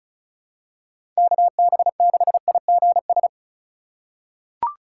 Morse code, heard as a keyed tone, about 34 wpm, 700 Hz. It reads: KB6IGS
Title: USA Call Signs - 6 Characters - 2x Character Spacing - 35wpm